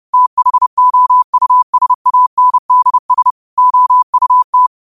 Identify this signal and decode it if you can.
Morse code, heard as a keyed tone, about 30 wpm, 1000 Hz. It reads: THOUSANDS OUT
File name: qotdfiles / 30wpm